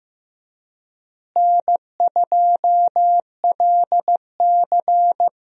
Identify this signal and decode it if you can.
Morse code, heard as a keyed tone, about 15 wpm, 700 Hz. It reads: N2LC
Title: USA Call Signs - 4 Characters - Repeated 3x - 15wpm